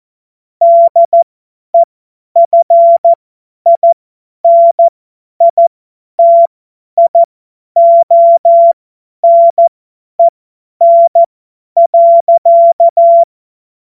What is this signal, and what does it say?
Morse code, heard as a keyed tone, about 14 wpm, spaced as for 10 wpm, 680 Hz. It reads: DEFINITIONEN.
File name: Morsekurs / Witze / Witz 14. Juni 2022 50 F70